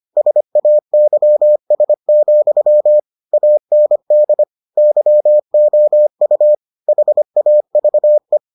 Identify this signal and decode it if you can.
Morse code, heard as a keyed tone, about 25 wpm, 600 Hz. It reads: SAYS, AND YOU HAVE